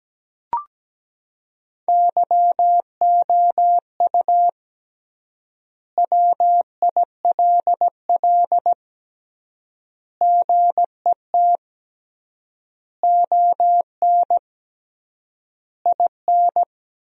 Morse code, heard as a keyed tone, about 17 wpm, 700 Hz. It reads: YOU WILL GET ON IN
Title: 3x Word-Spacing - Sentences from Top 200 Words - 17wpm